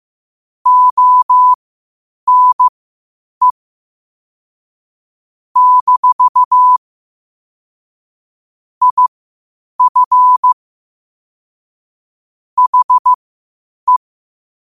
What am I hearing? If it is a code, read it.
Morse code, heard as a keyed tone, about 15 wpm, spaced as for 8 wpm, 1000 Hz. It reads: ONE - IF HE
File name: qotdfiles / 8wpm